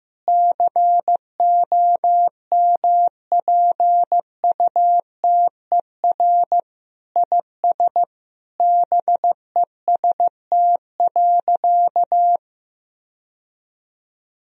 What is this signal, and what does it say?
Morse code, heard as a keyed tone, about 15 wpm, 700 Hz. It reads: COMPUTER IS BEST.
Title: Sentences from Top 1000 Words - Review Part 3 - 15wpm